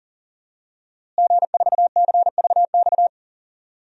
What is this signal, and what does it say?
Morse code, heard as a keyed tone, about 40 wpm, 700 Hz. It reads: G4CVX
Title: DXCC-100 Call Signs - 5 Characters - Repeated 2x - 40wpm